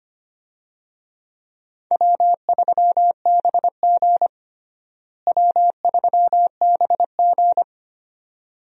Morse code, heard as a keyed tone, about 25 wpm, 700 Hz. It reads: W3BG W3BG
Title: USA Call Signs - 4 Characters - Repeated 2x - 25wpm